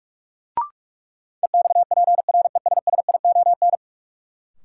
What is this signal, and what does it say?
Morse code, heard as a keyed tone, about 46 wpm, 700 Hz. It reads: EXPRESSION
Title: Sentences from Top 1200 Words - New Word Drill Alt - 45wpm